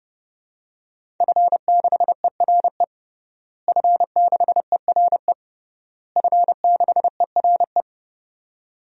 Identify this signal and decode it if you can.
Morse code, heard as a keyed tone, about 32 wpm, 700 Hz. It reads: F6ERE F6ERE F6ERE
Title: DXCC-100 Call Signs - 5 Characters - Repeated 3x - 30wpm